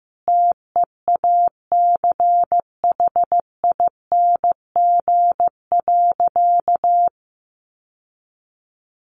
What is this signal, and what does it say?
Morse code, heard as a keyed tone, about 15 wpm, 700 Hz. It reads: TEACHING.
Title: Sentences from Top 1100 Words 15wpm